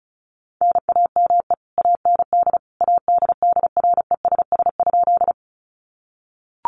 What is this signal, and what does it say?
Morse code, heard as a keyed tone, about 34 wpm, 700 Hz. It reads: NAME AND ADDRESS?
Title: Sentences from Top 1000 Words 35wpm